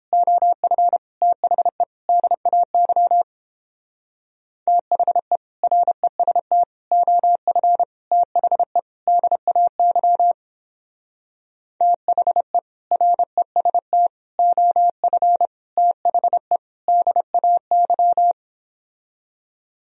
Morse code, heard as a keyed tone, about 29 wpm, 700 Hz. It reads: OF THE DAY THE REST OF THE DAY THE REST OF THE DAY